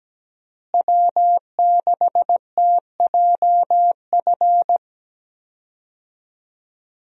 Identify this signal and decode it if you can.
Morse code, heard as a keyed tone, about 17 wpm, 700 Hz. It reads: W6TJF
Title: USA Call Signs - 5 Characters - Repeated 3x - 17wpm